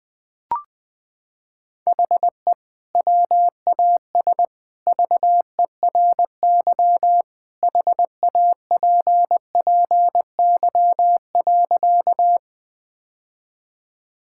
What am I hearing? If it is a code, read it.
Morse code, heard as a keyed tone, about 21 wpm, 700 Hz. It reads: HE WAS VERY HAPPY.
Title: Sentences with 2-Word Phrases - 20wpm